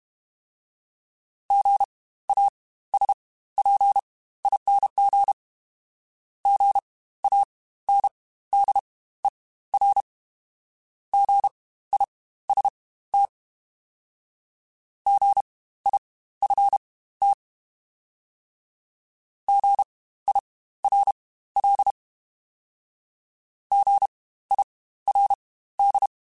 Morse code, heard as a keyed tone, about 32 wpm, spaced as for 16 wpm, 800 Hz. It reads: GASPING GANDER GIST GIFT GIRL GIRD
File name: Spaced Alphas / Spaced Alphas mp3s / Gs sp @30